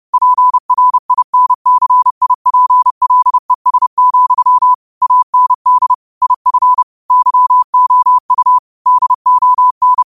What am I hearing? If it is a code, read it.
Morse code, heard as a keyed tone, about 30 wpm, 1000 Hz. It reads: PRINCIPLES, AND IF YOU DON
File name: qotdfiles / 30wpm